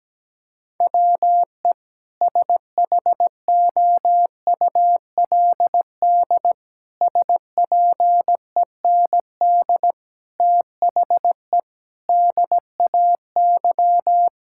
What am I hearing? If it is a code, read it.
Morse code, heard as a keyed tone, about 17 wpm, 700 Hz. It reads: WE SHOULD SPEND THE DAY
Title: Sentences from Top 1000 Words - Review Part 5 - 17wpm